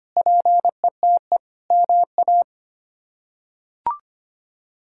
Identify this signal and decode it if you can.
Morse code, heard as a keyed tone, about 25 wpm, 700 Hz. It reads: PETE MA
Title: K1USN SST QSO Practice - 25wpm